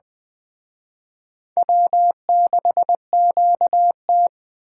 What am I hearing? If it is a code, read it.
Morse code, heard as a keyed tone, about 21 wpm, 700 Hz. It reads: W6QT
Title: Speed Racing - USA Call Signs 17wpm